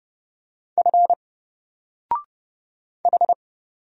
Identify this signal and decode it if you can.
Morse code, heard as a keyed tone, about 30 wpm, 700 Hz. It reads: F H
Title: Single Character (Letter:Number:Punctuation:Prosign) 30wpm